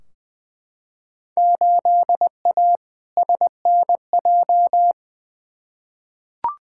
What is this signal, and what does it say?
Morse code, heard as a keyed tone, about 20 wpm, 700 Hz. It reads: 8A SNJ
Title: Field Day Exchanges Encore - 20wpm